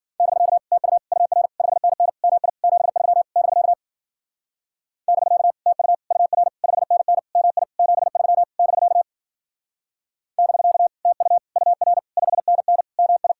Morse code, heard as a keyed tone, about 48 wpm, 700 Hz. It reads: <BK> TU UR 5NN MI 73 <BK> <BK> TU UR 5NN MI 73 <BK> <BK> TU UR 5NN MI
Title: Speed-Racing Common QSO Phrases - 40wpm